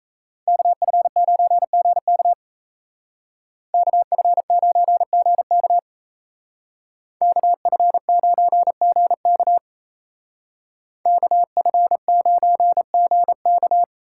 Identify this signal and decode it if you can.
Morse code, heard as a keyed tone, about 36 wpm, 700 Hz. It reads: KF9GK KF9GK KF9GK KF9GK